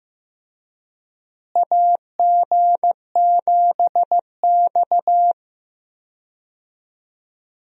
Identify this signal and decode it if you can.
Morse code, heard as a keyed tone, about 15 wpm, 700 Hz. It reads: AG7X
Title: USA Call Signs - 4 Characters - Repeated 3x - 15wpm